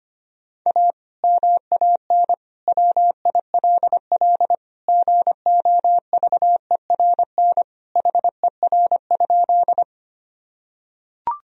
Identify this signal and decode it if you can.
Morse code, heard as a keyed tone, about 25 wpm, 700 Hz. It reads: A MAN WILL GOVERN HER?
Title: Sentences from Top 1000 Words - Review Part 2 - 25wpm